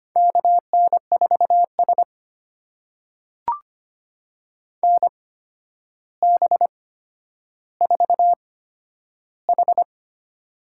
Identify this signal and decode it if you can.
Morse code, heard as a keyed tone, about 25 wpm, spaced as for 11 wpm, 700 Hz. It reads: KN4S NB4H
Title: USA Call Signs - 4 Characters - 8x Character Spacing - 25wpm